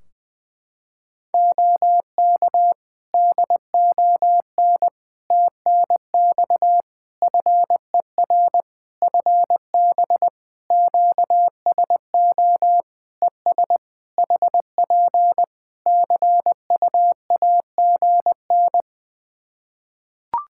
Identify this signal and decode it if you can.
Morse code, heard as a keyed tone, about 20 wpm, 700 Hz. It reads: OK DON TNX FER FB QSO ES HP CUAGN
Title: Practice QSO - Exchange 4 - Ending - 20wpm